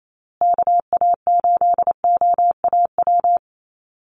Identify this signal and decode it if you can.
Morse code, heard as a keyed tone, about 28 wpm, 700 Hz. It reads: KA8OAW
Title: USA Call Signs Repeated 3x 28wpm